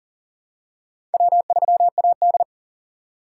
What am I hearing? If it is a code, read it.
Morse code, heard as a keyed tone, about 40 wpm, 700 Hz. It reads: W3AD
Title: USA Call Signs - 4 Characters - 2x Character Spacing - 40wpm